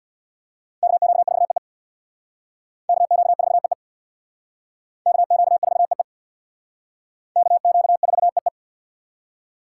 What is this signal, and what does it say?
Morse code, heard as a keyed tone, about 60 wpm, 700 Hz. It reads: KQ4EE KQ4EE KQ4EE KQ4EE
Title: Speed Racing - USA 2x2 Call Signs 50wpm